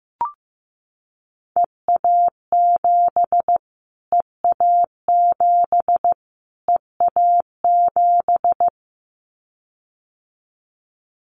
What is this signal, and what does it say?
Morse code, heard as a keyed tone, about 15 wpm, 700 Hz. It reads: EA7 EA7 EA7
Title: Callsign Prefixes for Top 25 DXCC - Repeated 3x - 15wpm